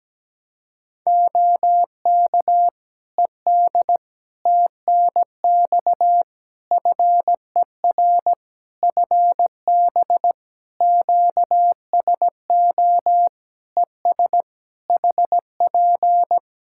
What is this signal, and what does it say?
Morse code, heard as a keyed tone, about 17 wpm, 700 Hz. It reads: OK ED TNX FER FB QSO ES HP
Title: K1USN SST QSO Practice 17wpm